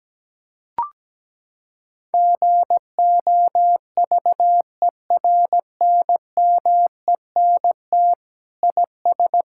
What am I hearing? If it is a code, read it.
Morse code, heard as a keyed tone, about 17 wpm, 700 Hz. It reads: GOVERNMENT IS
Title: Sentences from Top 1000 Words - Review Part 6 - 17wpm